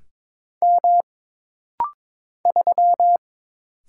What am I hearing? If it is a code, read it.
Morse code, heard as a keyed tone, about 22 wpm, 700 Hz. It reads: M 3
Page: This rapid-fire timing is geared to push your ICR (Instant Character Recognition) skills.